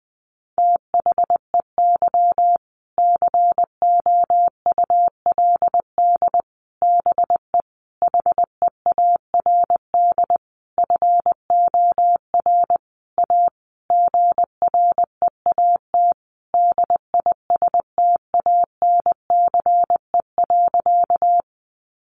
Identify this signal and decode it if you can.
Morse code, heard as a keyed tone, about 20 wpm, 700 Hz. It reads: THEY COULD BE HEARD FOR A GREAT DISTANCE.
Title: Sentences from Top 1200 Words 20wpm